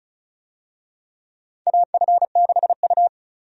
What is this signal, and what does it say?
Morse code, heard as a keyed tone, about 36 wpm, 700 Hz. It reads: AF6U
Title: USA Call Signs - 4 Characters - 35wpm